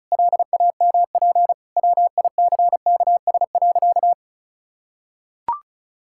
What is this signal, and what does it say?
Morse code, heard as a keyed tone, about 36 wpm, 700 Hz. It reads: LAMP WICKS.
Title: Pangram Challenge v2 - 35wpm